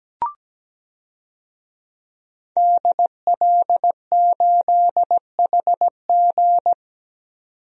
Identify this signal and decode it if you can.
Morse code, heard as a keyed tone, about 17 wpm, 700 Hz. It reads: DL8HG
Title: DXCC-100 Call Signs - 5 Characters - 17wpm